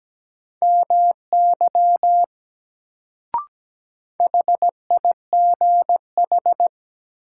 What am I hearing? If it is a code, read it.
Morse code, heard as a keyed tone, about 17 wpm, 700 Hz. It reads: MY HIGH
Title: Sets of 2 Words - New Word Drill - 17wpm